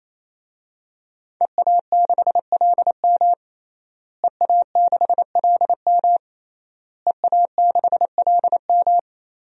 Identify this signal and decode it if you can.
Morse code, heard as a keyed tone, about 28 wpm, 700 Hz. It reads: EA6LM EA6LM EA6LM
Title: DXCC-100 Call Signs - 5 Characters - Repeated 3x - 28wpm